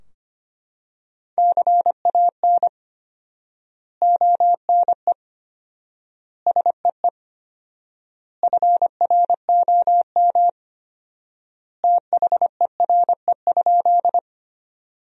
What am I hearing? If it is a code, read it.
Morse code, heard as a keyed tone, about 25 wpm, 700 Hz. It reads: CAN ONE SEE FROM THERE?
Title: 4x Word-Spacing - Sentences from Top 100 Words - 25wpm